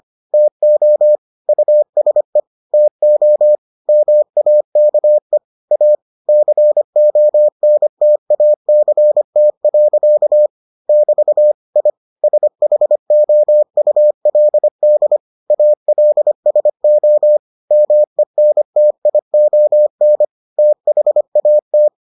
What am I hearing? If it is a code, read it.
Morse code, heard as a keyed tone, about 25 wpm, 600 Hz. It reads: TO USE TO MAKE A CONTACT. <BT> I SHOULD ALSO MENTION THAT